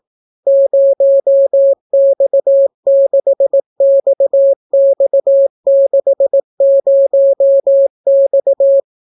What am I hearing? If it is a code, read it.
Morse code, heard as a keyed tone, about 18 wpm, 550 Hz. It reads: 0X6XX60X